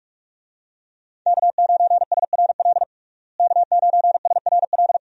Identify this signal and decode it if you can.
Morse code, heard as a keyed tone, about 42 wpm, 700 Hz. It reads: K9SRL K9SRL
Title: USA Call Signs - 5 Characters - Repeated 2x - 45wpm